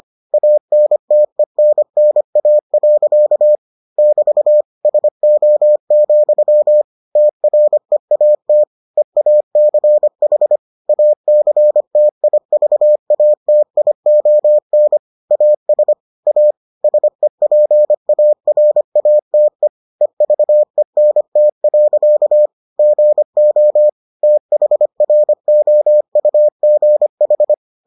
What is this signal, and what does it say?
Morse code, heard as a keyed tone, about 25 wpm, 600 Hz. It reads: ANTENNA. <BT> SO, TREAT EACH ACTIVATION AS A SEPARATE EVENT. GO THROUGH